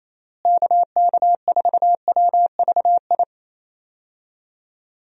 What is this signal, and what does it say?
Morse code, heard as a keyed tone, about 28 wpm, 700 Hz. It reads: KK4WVI
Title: USA Call Signs - 6 Characters - Repeated 3x - 28wpm